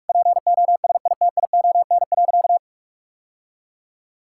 Morse code, heard as a keyed tone, about 41 wpm, 700 Hz. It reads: POSITION.
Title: Sentences from Top 1000 Words - Review Part 5 - 45wpm